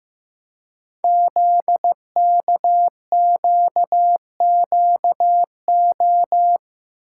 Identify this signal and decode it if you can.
Morse code, heard as a keyed tone, about 15 wpm, 700 Hz. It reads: ZKQQO
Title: Code Groups - 5 Letters - 15wpm